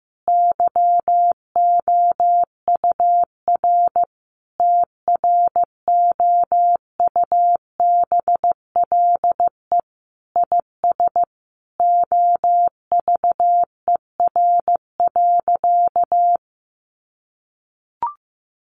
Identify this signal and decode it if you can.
Morse code, heard as a keyed tone, about 15 wpm, 700 Hz. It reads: YOUR TROUBLE IS OVER.
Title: Sentences from Top 1200 Words 15wpm